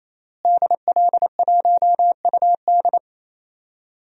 Morse code, heard as a keyed tone, about 27 wpm, 700 Hz. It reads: DL1UD
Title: DXCC-100 Call Signs - 5 Characters - Repeated 3x - 28wpm